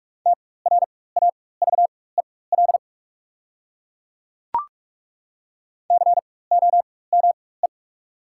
Morse code, heard as a keyed tone, about 42 wpm, spaced as for 21 wpm, 690 Hz. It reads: TRAVEL COME
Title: 4x Character Spacing - Top 500 Words - 45wpm